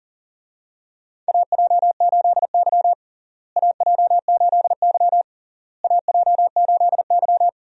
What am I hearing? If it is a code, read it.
Morse code, heard as a keyed tone, about 37 wpm, 700 Hz. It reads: AJ8Y AJ8Y AJ8Y
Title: USA Call Signs - 4 Characters - Repeated 3x - 40wpm